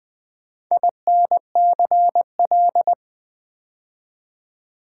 Morse code, heard as a keyed tone, about 20 wpm, 700 Hz. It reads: INCL
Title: Build A Word - 2 Letters At A Time - 8-Letter Words - 20wpm